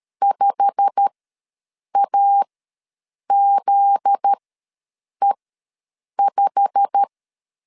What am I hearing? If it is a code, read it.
Morse code, heard as a keyed tone, about 13 wpm, spaced as for 7 wpm, 800 Hz. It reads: 5AZE5
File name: LEZIONI / 8 YJ45 / 8 YJ45 ws 12wpm 3146